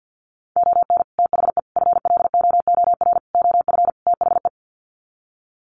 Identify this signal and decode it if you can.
Morse code, heard as a keyed tone, about 50 wpm, 700 Hz. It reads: ON THE FLOOR OF THE
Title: Sets of 5 Words 50wpm